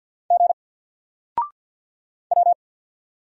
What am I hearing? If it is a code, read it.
Morse code, heard as a keyed tone, about 50 wpm, 700 Hz. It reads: G W
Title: Single Letter-Number Alt 50wpm